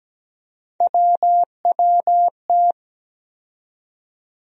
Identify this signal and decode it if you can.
Morse code, heard as a keyed tone, about 17 wpm, 700 Hz. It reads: WWT
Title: USA Call Sign Postfixes - 3 Letters 17wpm